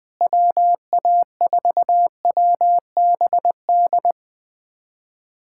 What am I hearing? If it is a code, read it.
Morse code, heard as a keyed tone, about 20 wpm, 700 Hz. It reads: WA4WBD
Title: USA Callsign DE Callsign - 6 Characters - 20wpm